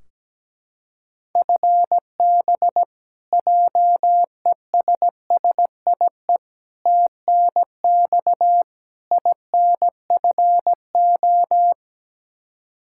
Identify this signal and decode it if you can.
Morse code, heard as a keyed tone, about 17 wpm, 700 Hz. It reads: FB JESSIE TNX INFO
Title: Practice QSO - Exchange 3 - More Info - 17wpm